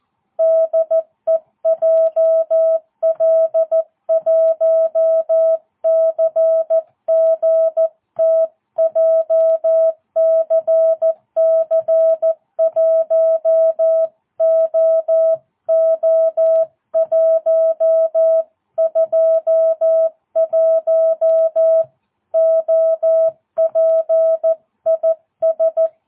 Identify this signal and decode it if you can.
Morse code, heard as a keyed tone, about 14 wpm, 650 Hz. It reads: DE JL1CGT JCC1OO121 OP IS
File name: jh8cbh / 2022cw